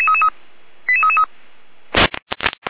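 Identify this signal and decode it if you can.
Five-tone selective call (ZVEI), heard as quick runs of five tones, tones of about 70 ms. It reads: 88333
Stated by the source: ZVEI five-tone call 88435 received on an SDR.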